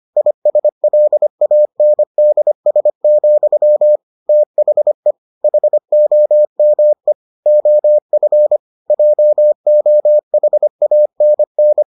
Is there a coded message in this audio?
Yes: Morse code from 0.2 to 11.8 s, heard as a keyed tone, about 25 wpm, 600 Hz. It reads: ISLANDS, THE HOME OF JOHANN